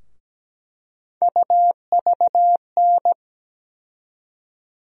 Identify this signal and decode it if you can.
Morse code, heard as a keyed tone, about 17 wpm, 700 Hz. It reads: UVN
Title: USA Call Sign Postfixes - 3 Letters 17wpm